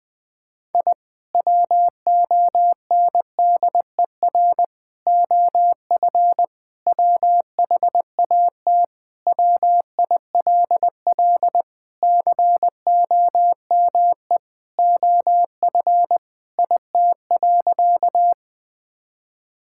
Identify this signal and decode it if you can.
Morse code, heard as a keyed tone, about 20 wpm, 700 Hz. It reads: I WONDER OF WHAT WILL COME OF IT.
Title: Sentences with 2-Word Phrases - 20wpm